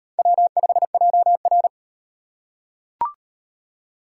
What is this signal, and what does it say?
Morse code, heard as a keyed tone, about 36 wpm, 700 Hz. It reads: W5JR